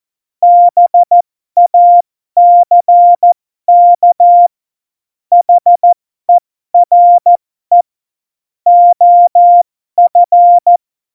Morse code, heard as a keyed tone, about 14 wpm, spaced as for 12 wpm, 700 Hz. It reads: BACK HERE OF
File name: Morsekurs / Witze